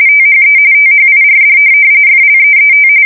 RTTY, about 45.45 baud. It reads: THE THIRD FLEET
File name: fsk170 / USS-New-Jersey